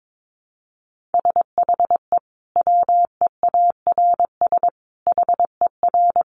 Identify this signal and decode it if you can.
Morse code, heard as a keyed tone, about 22 wpm, 700 Hz. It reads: SHE WEARS HER
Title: Easy Phrases 22wpm